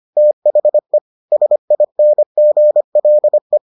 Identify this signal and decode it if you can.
Morse code, heard as a keyed tone, about 25 wpm, 600 Hz. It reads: THE SINGLE